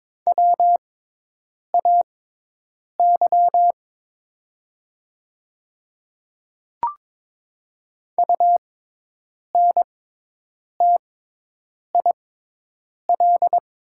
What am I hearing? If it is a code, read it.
Morse code, heard as a keyed tone, about 22 wpm, spaced as for 8 wpm, 700 Hz. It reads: WAY UNTIL
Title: 6x Character Spacing - Top 300 Words - 22wpm